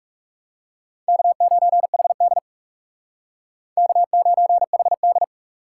Morse code, heard as a keyed tone, about 38 wpm, 700 Hz. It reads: K9HD K9HD